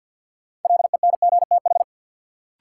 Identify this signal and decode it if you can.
Morse code, heard as a keyed tone, about 51 wpm, 700 Hz. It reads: LENGTH
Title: Sentences from Top 1000 Words - New Word Drill - 50wpm